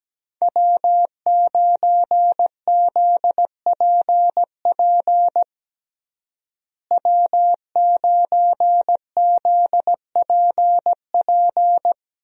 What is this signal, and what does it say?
Morse code, heard as a keyed tone, about 17 wpm, 700 Hz. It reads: W9ZPP W9ZPP
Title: USA Call Signs - 5 Characters - Repeated 3x - 17wpm